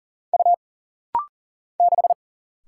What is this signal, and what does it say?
Morse code, heard as a keyed tone, about 40 wpm, 700 Hz. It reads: U 6
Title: Single Letter-Number - Rapid-Fire 40wpm